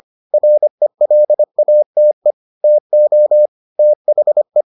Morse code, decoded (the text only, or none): RELATE TO THE